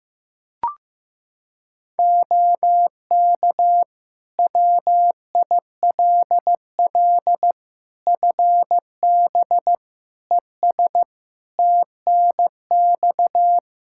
Morse code, heard as a keyed tone, about 15 wpm, 700 Hz. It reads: OK WILL FB ES TNX
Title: Practice QSO - Exchange 2 - Info - 15wpm